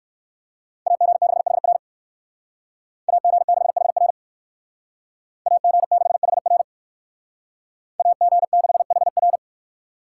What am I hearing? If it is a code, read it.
Morse code, heard as a keyed tone, about 68 wpm, 700 Hz. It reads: AG6HR AG6HR AG6HR AG6HR